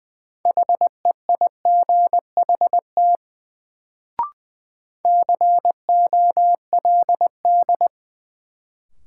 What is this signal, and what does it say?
Morse code, heard as a keyed tone, about 20 wpm, 700 Hz. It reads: HEIGHT COLD
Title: Sets of 5 Words - New Word Drill - 20wpm